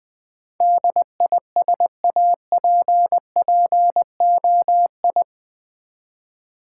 Morse code, heard as a keyed tone, about 20 wpm, 700 Hz. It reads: DISAPPOI
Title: Build A Word - 4 Letters At A Time - 10-Letter Words - 20wpm